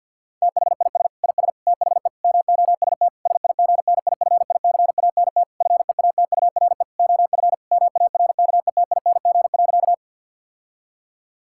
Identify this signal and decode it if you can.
Morse code, heard as a keyed tone, about 50 wpm, 700 Hz. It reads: THIS IS THE MOST SIGNIFICANT FEATURE OF MARKETING.
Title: Sentences from Top 1000 Words - Review Part 6 - 50wpm